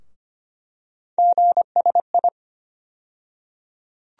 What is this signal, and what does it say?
Morse code, heard as a keyed tone, about 25 wpm, 700 Hz. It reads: GSI
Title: USA Call Sign Postfixes - 3 Letters 25wpm